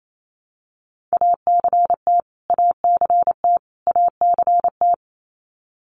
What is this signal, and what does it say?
Morse code, heard as a keyed tone, about 28 wpm, 700 Hz. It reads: ACT ACT ACT
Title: Top 500 Words Repeated 28wpm